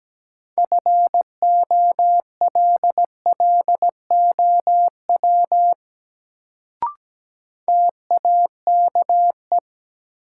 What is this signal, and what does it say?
Morse code, heard as a keyed tone, about 17 wpm, 700 Hz. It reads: FOLLOW TAKE
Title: Sentences from Top 200 Words - New Word Drill - 17wpm